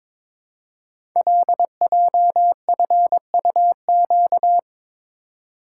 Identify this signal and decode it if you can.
Morse code, heard as a keyed tone, about 22 wpm, 700 Hz. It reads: LJFUQ
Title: Code Groups - 5 Letters - 22wpm